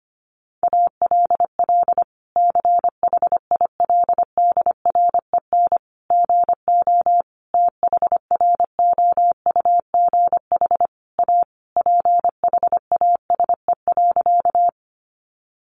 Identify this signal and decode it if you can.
Morse code, heard as a keyed tone, about 25 wpm, 700 Hz. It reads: ALL CHILDREN GO THROUGH A PHASE.
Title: Sentences from Top 1200 Words 25wpm